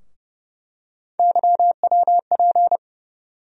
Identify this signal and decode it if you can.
Morse code, heard as a keyed tone, about 31 wpm, 700 Hz. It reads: YWP
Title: USA Call Sign Postfixes - 3 Letters 30wpm